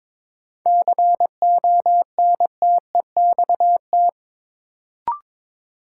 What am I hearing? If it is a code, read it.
Morse code, heard as a keyed tone, about 22 wpm, 700 Hz. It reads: CONTEXT
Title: Sentences from Top 1000 Words - Review Part 7 - New Word Drill - 22wpm